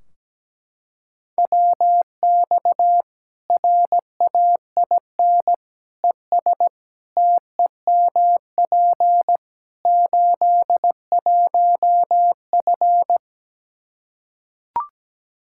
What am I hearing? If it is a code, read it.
Morse code, heard as a keyed tone, about 17 wpm, 700 Hz. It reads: WX RAIN ES TEMP 81F
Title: Practice QSO - Exchange 2 - Info - 17wpm